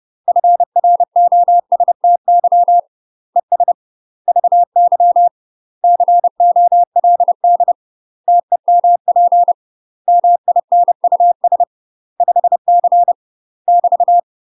Morse code, heard as a keyed tone, about 30 wpm, 700 Hz. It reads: FROSTY ES VY COLD TEMP MINUS 5C <BT>
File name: UK MASTER QSOs TRAINING / 30WPM / UK MASTER QSOs 0014